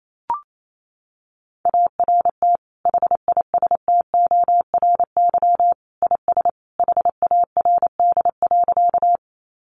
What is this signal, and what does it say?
Morse code, heard as a keyed tone, about 28 wpm, 700 Hz. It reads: ART HISTORY IS HARD.
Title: Sentences from Top 600 Words 28wpm